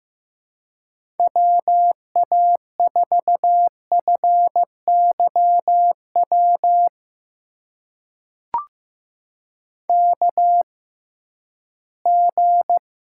Morse code, heard as a keyed tone, about 15 wpm, spaced as for 11 wpm, 700 Hz. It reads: WA4FYW KG
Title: USA Call Signs - 6 Characters - 6x Character Spacing - 15wpm